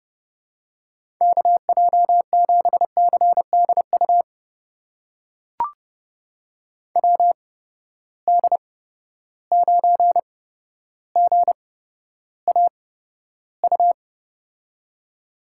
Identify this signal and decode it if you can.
Morse code, heard as a keyed tone, about 30 wpm, spaced as for 13 wpm, 700 Hz. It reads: KJ7CDU WD9GAU
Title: USA Call Signs - 6 Characters - 8x Character Spacing - 30wpm